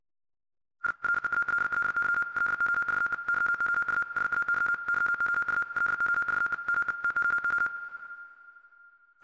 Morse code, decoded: I REALLY LOVE YOU SAY YOU LOVE ME TOO